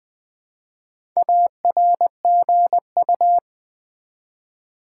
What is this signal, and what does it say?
Morse code, heard as a keyed tone, about 20 wpm, 700 Hz. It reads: ARGU
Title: Build A Word - 1 Letter At A Time - 5-Letter Words - 20wpm